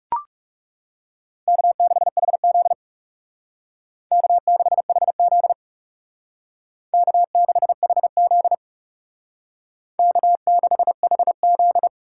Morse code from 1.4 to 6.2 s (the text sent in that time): K6HZ K6HZ